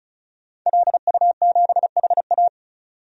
Morse code, heard as a keyed tone, about 36 wpm, 700 Hz. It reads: LU7HA
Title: DXCC-100 Call Signs - 5 Characters - 35wpm